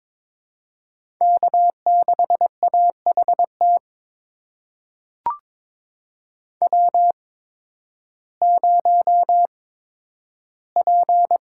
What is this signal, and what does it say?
Morse code, heard as a keyed tone, about 22 wpm, spaced as for 12 wpm, 700 Hz. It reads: K6AHT W0P
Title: USA Call Signs - 8x Character Spacing - 22wpm